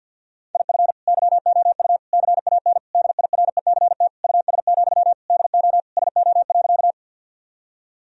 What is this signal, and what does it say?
Morse code, heard as a keyed tone, about 50 wpm, 700 Hz. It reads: IF YOU CAN DIRECT US, DO SO.
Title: Sentences from Top 1000 Words - Review Part 2 - 50wpm